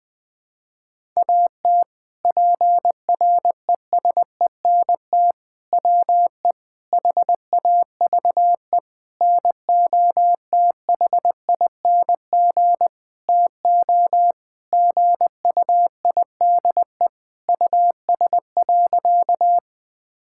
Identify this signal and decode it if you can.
Morse code, heard as a keyed tone, about 20 wpm, 700 Hz. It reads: AT PRESENT WE HAVE NOTHING TO GUIDE US.
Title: Sentences from Top 1000 Words - Review Part 5 - 20wpm